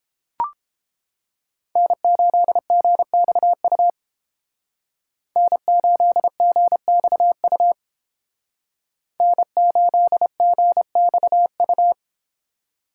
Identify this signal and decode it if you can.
Morse code, heard as a keyed tone, about 30 wpm, 700 Hz. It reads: N8GXU N8GXU N8GXU